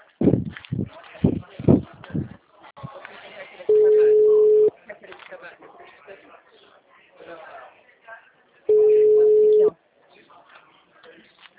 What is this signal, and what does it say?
Ringback tone (Europe, 425 Hz), with speech.